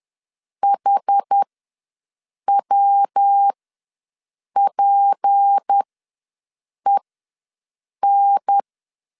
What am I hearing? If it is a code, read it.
Morse code, heard as a keyed tone, about 10 wpm, spaced as for 6 wpm, 800 Hz. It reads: HWPEN
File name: LEZIONI / 7 GF9Z / 7 G ws 10wpm 3843